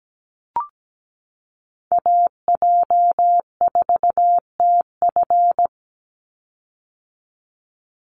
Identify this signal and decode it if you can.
Morse code, heard as a keyed tone, about 17 wpm, 700 Hz. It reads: AJ4TF
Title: USA Call Signs 17wpm